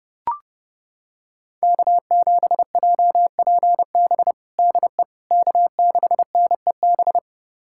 Morse code, heard as a keyed tone, about 32 wpm, 700 Hz. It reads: K7JPB DE K6NEB
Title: USA Callsign DE Callsign 30wpm